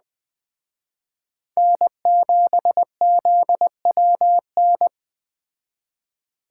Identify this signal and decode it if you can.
Morse code, heard as a keyed tone, about 20 wpm, 700 Hz. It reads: N7ZWN
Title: Speed Racing - USA 1x3 Call Signs 17wpm